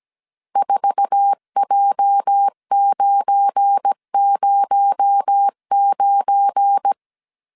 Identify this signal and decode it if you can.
Morse code, heard as a keyed tone, about 17 wpm, 800 Hz. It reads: 4J909
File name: CW / DET / 1 9 1500C S C 16wpm 2909